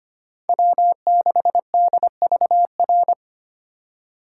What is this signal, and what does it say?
Morse code, heard as a keyed tone, about 25 wpm, 700 Hz. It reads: W6DVR
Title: USA Call Signs - 5 Characters - 4x Character Spacing - 25wpm